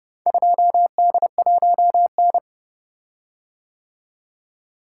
Morse code, heard as a keyed tone, about 29 wpm, 700 Hz. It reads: 2D1N